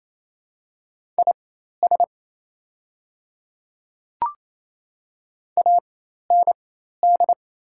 Morse code, heard as a keyed tone, about 28 wpm, spaced as for 13 wpm, 690 Hz. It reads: IS AND
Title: 4x Character Spacing - Top 100 Words - 28wpm